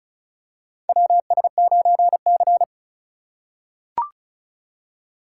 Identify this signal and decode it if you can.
Morse code, heard as a keyed tone, about 34 wpm, 700 Hz. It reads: WS9C
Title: USA Call Signs - 4 Characters - 4x Character Spacing - 35wpm